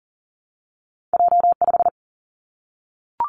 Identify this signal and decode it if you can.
Morse code, heard as a keyed tone, about 40 wpm, 700 Hz. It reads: J 5
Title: Two Challenging Letter-Number Combinations 40wpm